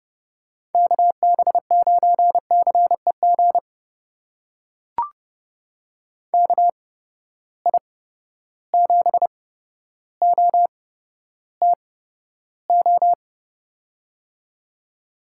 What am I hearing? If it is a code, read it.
Morse code, heard as a keyed tone, about 31 wpm, spaced as for 13 wpm, 700 Hz. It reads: KB9CEG KI7OTO
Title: USA Call Signs - 6 Characters - 8x Character Spacing - 30wpm